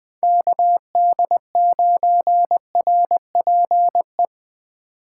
Morse code, heard as a keyed tone, about 20 wpm, 700 Hz. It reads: KD9RPE
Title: 2023 POTA Hunter Call Signs - 20wpm